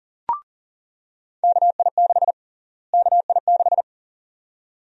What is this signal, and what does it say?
Morse code, heard as a keyed tone, about 40 wpm, 700 Hz. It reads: KI6 KI6
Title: USA Call Sign Prefixes - Repeated 2x - 40wpm